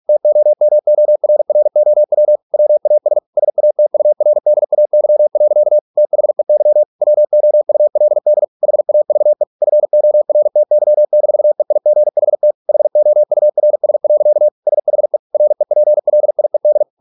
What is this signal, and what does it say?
Morse code, heard as a keyed tone, about 46 wpm, 600 Hz. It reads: TOMORROW WAS SATURDAY. THEY WOULD HAVE FORTY-EIGHT HOURS. SHE REPLIED